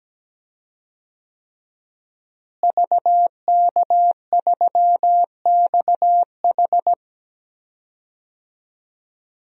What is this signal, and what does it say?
Morse code, heard as a keyed tone, about 17 wpm, 700 Hz. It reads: VK3XH
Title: DXCC-100 Call Signs - 5 Characters - Repeated 3x - 17wpm